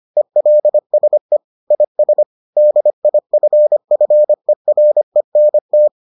Morse code, heard as a keyed tone, about 25 wpm, 600 Hz. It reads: ELSE IS DIFFERENT